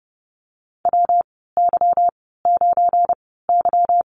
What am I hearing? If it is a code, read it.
Morse code, heard as a keyed tone, about 30 wpm, spaced as for 17 wpm, 700 Hz. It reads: WY9Y
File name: Lesson 37g - Letter x - Call Signs 30wpm